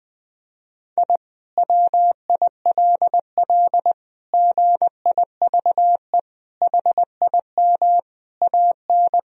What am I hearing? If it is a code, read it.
Morse code, heard as a keyed tone, about 20 wpm, 700 Hz. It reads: I WILL GIVE HIM AN
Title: Sentences from Top 1000 Words - Review Part 4 - 20wpm